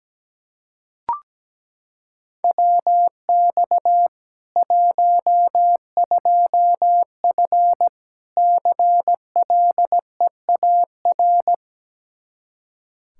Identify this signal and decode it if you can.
Morse code, heard as a keyed tone, about 17 wpm, 700 Hz. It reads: WX 12F CLEAR
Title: Practice QSO - Exchange 2 - Info - 17wpm